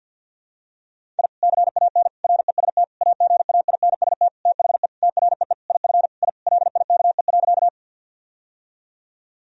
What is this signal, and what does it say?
Morse code, heard as a keyed tone, about 51 wpm, 700 Hz. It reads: I CAN REST AGAINST THE TREE IF I LIKE.
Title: Sentences from Top 1000 Words - Review Part 1 - 50wpm